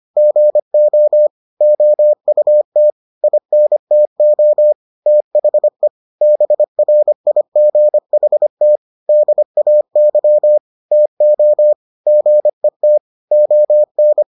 Morse code, heard as a keyed tone, about 25 wpm, 600 Hz. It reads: GO OUT INTO THE BRIGHT DAY TO GET ON